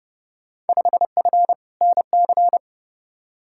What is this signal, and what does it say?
Morse code, heard as a keyed tone, about 30 wpm, 700 Hz. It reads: 5F NC
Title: Field Day Exchanges Encore - 30wpm